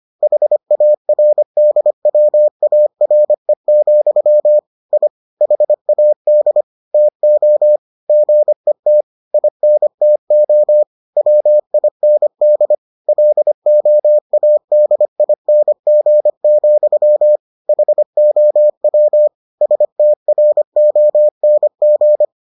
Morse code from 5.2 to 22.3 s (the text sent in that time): HAD TO GET INTO WIND LOADING, HOW STRONG